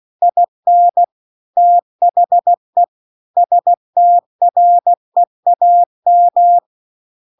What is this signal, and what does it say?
Morse code, heard as a keyed tone, about 16 wpm, 700 Hz. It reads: IN THE STREAM